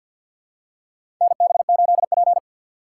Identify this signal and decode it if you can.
Morse code, heard as a keyed tone, about 52 wpm, 700 Hz. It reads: NB8P
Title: USA Call Signs - 2x Character Spacing - 50wpm